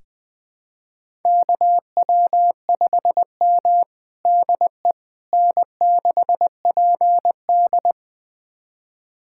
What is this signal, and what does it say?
Morse code, heard as a keyed tone, about 20 wpm, 700 Hz. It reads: KW5M DE N6PD
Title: USA Callsign DE Callsign - 4 Characters - 20wpm